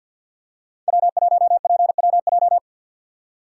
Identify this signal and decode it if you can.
Morse code, heard as a keyed tone, about 51 wpm, 700 Hz. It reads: W1PWJ
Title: USA Call Signs - 5 Characters - 3x Character Spacing - 50wpm